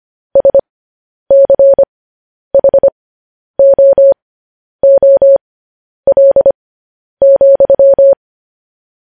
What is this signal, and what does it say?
Morse code, heard as a keyed tone, about 25 wpm, spaced as for 10 wpm, 550 Hz. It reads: SCHOOL,